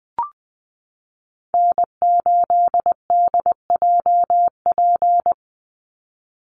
Morse code, heard as a keyed tone, about 19 wpm, 700 Hz. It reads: N8DJP
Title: USA Call Signs 20wpm